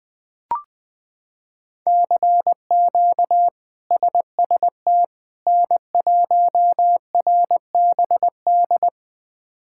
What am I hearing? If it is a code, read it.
Morse code, heard as a keyed tone, about 20 wpm, 700 Hz. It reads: CQ SST N1RBD
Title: K1USN SST QSO Practice - 20wpm